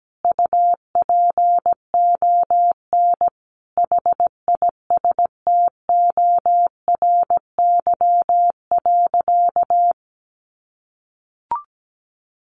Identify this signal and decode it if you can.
Morse code, heard as a keyed tone, about 17 wpm, 700 Hz. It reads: UPON HISTORY.
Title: Sentences from Top 1100 Words 17wpm